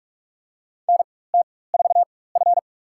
Morse code, heard as a keyed tone, about 46 wpm, spaced as for 21 wpm, 700 Hz. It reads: NT4F
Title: USA Call Signs - 4 Characters - 4x Character Spacing - 45wpm